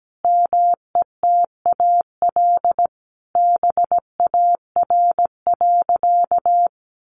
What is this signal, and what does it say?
Morse code, heard as a keyed tone, about 17 wpm, 700 Hz. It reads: METAL BAR.
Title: Additional Easy Phrases 17wpm